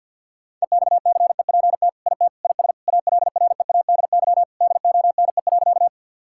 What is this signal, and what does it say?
Morse code, heard as a keyed tone, about 50 wpm, 700 Hz. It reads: EXCEPT IT IS ALREADY DONE.
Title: Sentences from Top 1600 Words 50wpm